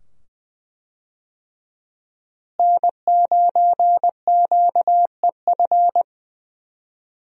Morse code, heard as a keyed tone, about 20 wpm, 700 Hz. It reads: N9QEF
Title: USA Call Signs - 5 Characters - Repeated 2x - 20wpm